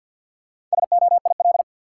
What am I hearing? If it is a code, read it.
Morse code, heard as a keyed tone, about 52 wpm, 700 Hz. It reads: SOIL
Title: Sentences from Top 1000 Words - Review Part 4 - New Word Drill - 50wpm